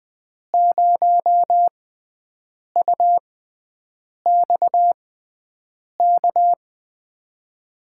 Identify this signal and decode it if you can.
Morse code, heard as a keyed tone, about 20 wpm, spaced as for 7 wpm, 700 Hz. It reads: 0UXK
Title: USA Call Signs - 6 Characters - 6x Character Spacing - 20wpm